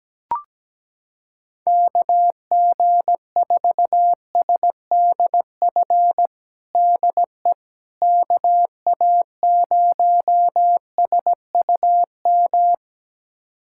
Morse code, heard as a keyed tone, about 17 wpm, 700 Hz. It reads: KG4SDF DE KA0SUM
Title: USA Callsign DE Callsign - 6 Characters - 17wpm